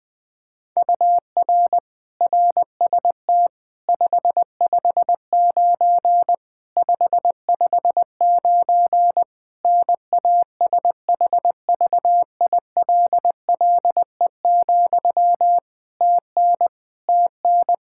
Morse code, heard as a keyed tone, about 20 wpm, 700 Hz. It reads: UR RST 559 559 NASHVILLE, TN TN
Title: Practice QSO - Exchange 1 - Beginning - 20wpm